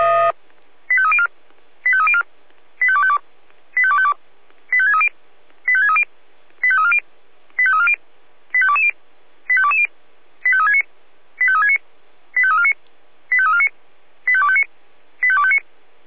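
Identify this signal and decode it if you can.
Five-tone selective call (ZVEI), heard as quick runs of five tones, tones of about 70 ms. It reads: 85304 85304 85252 85252 85630 85630 85340 85340 85220 85220 85369 85369 85349 85349 85259 85259
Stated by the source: ZVEI five-tone selective call 85360 received by SDR.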